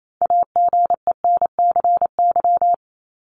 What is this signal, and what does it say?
Morse code, heard as a keyed tone, about 28 wpm, 700 Hz. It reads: AGENCY
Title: Top 1001-1250 Words 28wpm